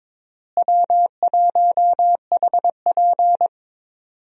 Morse code, heard as a keyed tone, about 21 wpm, 700 Hz. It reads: W1HP
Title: USA Call Signs - 4 Characters - Repeated 3x - 22wpm